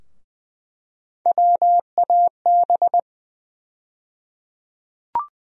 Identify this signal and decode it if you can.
Morse code, heard as a keyed tone, about 20 wpm, 700 Hz. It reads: WAB
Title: USA Call Sign Postfixes - 3 Letters 20wpm